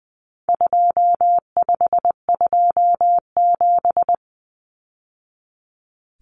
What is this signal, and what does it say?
Morse code, heard as a keyed tone, about 20 wpm, 700 Hz. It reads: 2527
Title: Numbers - 4-Digit 20wpm